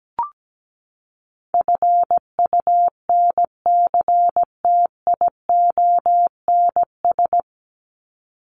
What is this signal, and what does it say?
Morse code, heard as a keyed tone, about 17 wpm, 700 Hz. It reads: FUNCTIONS
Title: ES:S Suffix - Top 100 Words with Base Words 17wpm